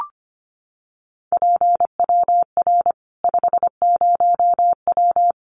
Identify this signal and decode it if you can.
Morse code, heard as a keyed tone, about 25 wpm, 700 Hz. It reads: PWR 50W
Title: Common QSO Phrases 25wpm